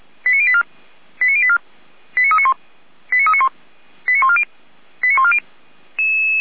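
Five-tone selective call (ZVEI), heard as quick runs of five tones, tones of about 70 ms. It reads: 89084 89084 89391 89391 89140 89140
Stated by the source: ZVEI five-tone call 89320 received by SDR.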